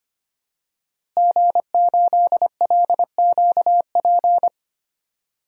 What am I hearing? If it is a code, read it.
Morse code, heard as a keyed tone, about 25 wpm, 700 Hz. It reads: G8LQP
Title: DXCC-100 Call Signs - 5 Characters - Repeated 2x - 25wpm